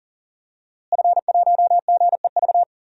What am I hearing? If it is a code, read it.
Morse code, heard as a keyed tone, about 40 wpm, 700 Hz. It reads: F1GEV
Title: DXCC-100 Call Signs - 5 Characters - Repeated 3x - 40wpm